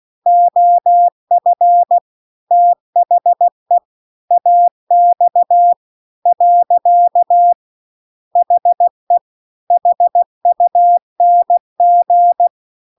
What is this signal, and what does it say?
Morse code, heard as a keyed tone, about 16 wpm, 700 Hz. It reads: OF THE AX . HE HUNG